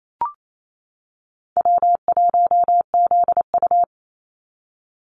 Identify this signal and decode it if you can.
Morse code, heard as a keyed tone, about 28 wpm, 700 Hz. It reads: W1ZU
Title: NA SOTA Activator Callsigns 28wpm